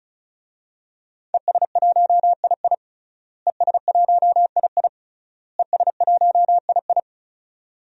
Morse code, heard as a keyed tone, about 35 wpm, 700 Hz. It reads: ES1II ES1II ES1II
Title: DXCC-100 Call Signs - 5 Characters - Repeated 3x - 35wpm